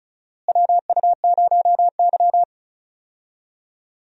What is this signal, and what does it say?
Morse code, heard as a keyed tone, about 36 wpm, 700 Hz. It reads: WU0Y
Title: USA Call Signs - 4 Characters - Repeated 2x - 35wpm